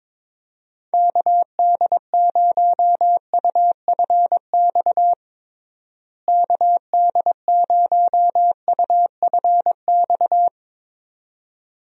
Morse code, heard as a keyed tone, about 22 wpm, 700 Hz. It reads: KD0UFX KD0UFX
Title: USA Call Signs - 6 Characters - Repeated 2x - 22wpm